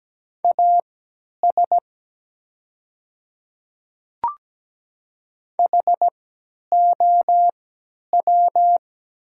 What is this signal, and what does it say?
Morse code, heard as a keyed tone, about 17 wpm, spaced as for 10 wpm, 690 Hz. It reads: AS HOW
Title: 3x Character Spacing - Top 100 Words - 17wpm